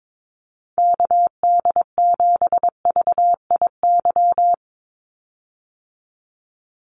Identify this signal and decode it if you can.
Morse code, heard as a keyed tone, about 22 wpm, 700 Hz. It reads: KD7VIY
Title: USA Call Signs 22wpm